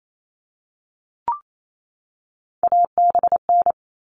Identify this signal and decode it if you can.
Morse code, heard as a keyed tone, about 28 wpm, 700 Hz. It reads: ABN
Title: All Cut Numbers - 3-Digit 28wpm